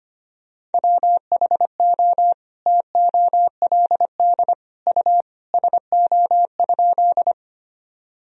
Morse code, heard as a keyed tone, about 25 wpm, 700 Hz. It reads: WHO TOLD U SO?
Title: Sentences from Top 400 Words and Abbreviations 25wpm